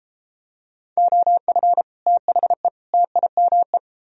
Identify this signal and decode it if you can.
Morse code, heard as a keyed tone, about 32 wpm, 700 Hz. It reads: OF THE TIME